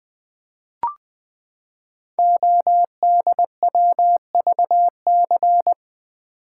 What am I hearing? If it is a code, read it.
Morse code, heard as a keyed tone, about 20 wpm, 700 Hz. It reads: ODWVC
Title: Code Groups - 5 Letters - 20wpm